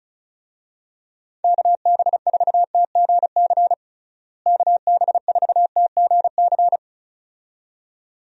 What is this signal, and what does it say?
Morse code, heard as a keyed tone, about 36 wpm, 700 Hz. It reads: KB4TGC KB4TGC
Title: USA Call Signs - 6 Characters - Repeated 2x - 35wpm